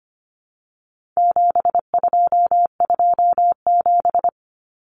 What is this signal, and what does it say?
Morse code, heard as a keyed tone, about 25 wpm, 700 Hz. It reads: 7227
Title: Common Cut Numbers - 4-Digit 25wpm